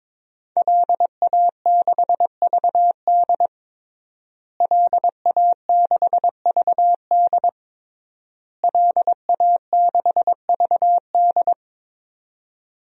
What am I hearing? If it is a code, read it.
Morse code, heard as a keyed tone, about 22 wpm, 700 Hz. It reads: LA6VD LA6VD LA6VD
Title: DXCC-100 Call Signs - 5 Characters - Repeated 3x - 22wpm